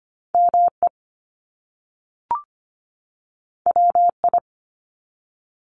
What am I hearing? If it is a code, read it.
Morse code, heard as a keyed tone, about 25 wpm, 700 Hz. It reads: ME WI
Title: US States 25wpm